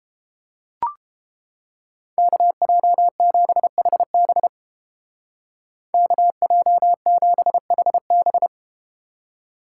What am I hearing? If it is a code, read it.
Morse code, heard as a keyed tone, about 32 wpm, 700 Hz. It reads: KJ7HB KJ7HB